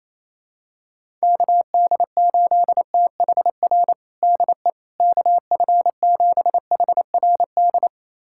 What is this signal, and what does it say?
Morse code, heard as a keyed tone, about 28 wpm, 700 Hz. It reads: KD8THR DE KF7HRD
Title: USA Callsign DE Callsign - 6 Characters - 28wpm